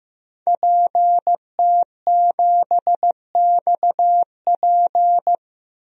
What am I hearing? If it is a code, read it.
Morse code, heard as a keyed tone, about 15 wpm, 700 Hz. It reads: PT7XP
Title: DXCC-100 Call Signs - 5 Characters - 15wpm